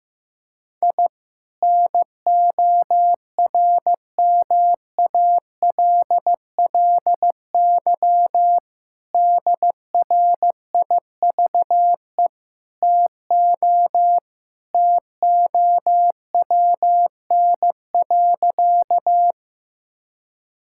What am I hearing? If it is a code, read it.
Morse code, heard as a keyed tone, about 15 wpm, 700 Hz. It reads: I NORMALLY DRIVE TO TOWN.
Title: Sentences from Top 1000 Words - Review Part 6 - 15wpm